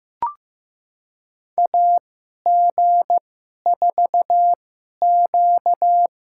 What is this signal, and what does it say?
Morse code, heard as a keyed tone, about 15 wpm, spaced as for 11 wpm, 700 Hz. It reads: AG4Q
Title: USA Call Signs - 2x Character Spacing - 15wpm